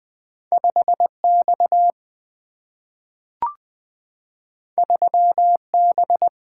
Morse code, heard as a keyed tone, about 20 wpm, 700 Hz. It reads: 5X 3B
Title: Single-Letter Single-Number 20wpm